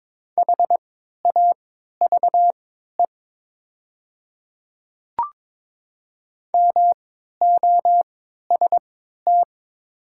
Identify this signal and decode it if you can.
Morse code, heard as a keyed tone, about 22 wpm, spaced as for 13 wpm, 700 Hz. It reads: HAVE MOST
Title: 3x Character Spacing - Top 100 Words - 22wpm